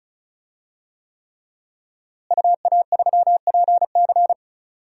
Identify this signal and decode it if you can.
Morse code, heard as a keyed tone, about 36 wpm, 700 Hz. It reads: UA3PC
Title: DXCC-100 Call Signs - 5 Characters - Repeated 3x - 35wpm